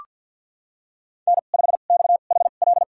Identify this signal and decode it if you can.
Morse code, heard as a keyed tone, about 52 wpm, spaced as for 37 wpm, 690 Hz. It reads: N5XHL
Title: USA Call Signs - 5 Characters - 2x Character Spacing - 50wpm